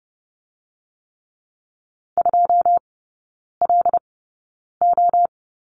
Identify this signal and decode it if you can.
Morse code, heard as a keyed tone, about 30 wpm, spaced as for 9 wpm, 700 Hz. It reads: 2LO
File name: 081 Lesson 14e - Number 2 - Three Characters 30wpm